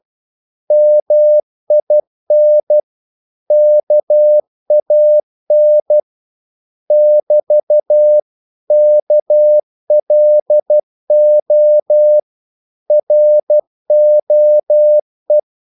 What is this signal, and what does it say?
Morse code, heard as a keyed tone, about 12 wpm, 600 Hz. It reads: MIN KAN <BT> KLO ROE
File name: Korte ord DK 12wpm